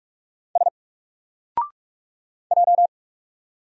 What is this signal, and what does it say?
Morse code, heard as a keyed tone, about 44 wpm, 700 Hz. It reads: S J
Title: Single Letter-Number 45wpm